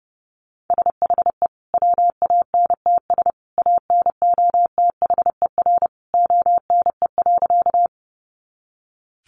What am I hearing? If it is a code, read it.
Morse code, heard as a keyed tone, about 30 wpm, 700 Hz. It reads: SHE WANTS ANOTHER ONE.
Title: Easy Phrases 30wpm